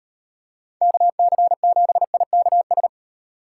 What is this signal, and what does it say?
Morse code, heard as a keyed tone, about 38 wpm, 700 Hz. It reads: KC7IKS